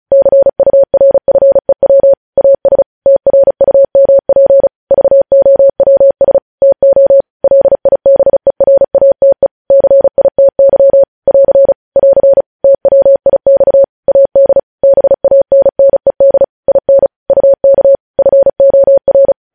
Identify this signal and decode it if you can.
Morse code, heard as a keyed tone, about 36 wpm, 550 Hz. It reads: CURFEW AS TRUMP VOWS TO LIBERATE CITY <AR> <AR> TWIX AD BANNED IN UK FOR
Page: news headlines in morse code at 35 wpm